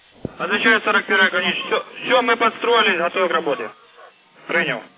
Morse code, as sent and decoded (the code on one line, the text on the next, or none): none